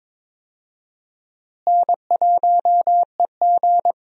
Morse code, heard as a keyed tone, about 22 wpm, 700 Hz. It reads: N1EG
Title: USA Call Signs - 4 Characters - Repeated 3x - 22wpm